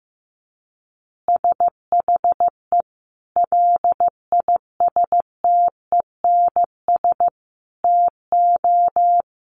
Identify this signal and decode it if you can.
Morse code, heard as a keyed tone, about 15 wpm, 700 Hz. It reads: SHE LISTENS TO
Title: Easy Phrases 15wpm